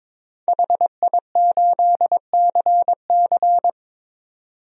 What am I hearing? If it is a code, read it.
Morse code, heard as a keyed tone, about 22 wpm, 700 Hz. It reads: HI8CC
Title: DXCC-100 Call Signs - 5 Characters - 22wpm